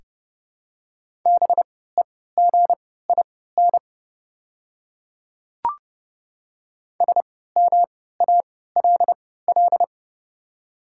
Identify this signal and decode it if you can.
Morse code, heard as a keyed tone, about 30 wpm, spaced as for 17 wpm, 700 Hz. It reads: BEGIN SMALL
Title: 3x Character Spacing - Top 500 Words - 30wpm